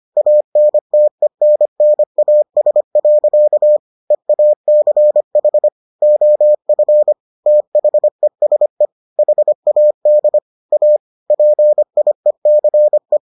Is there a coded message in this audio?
Yes: Morse code from 0.2 to 13.2 s, heard as a keyed tone, about 25 wpm, 600 Hz. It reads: ANTENNAS. EACH OF THESE HAD A PIECE